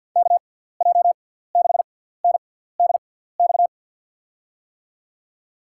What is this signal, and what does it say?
Morse code, heard as a keyed tone, about 50 wpm, spaced as for 17 wpm, 700 Hz. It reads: KJ6NDX
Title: USA Call Signs - 6 Characters - 6x Character Spacing - 50wpm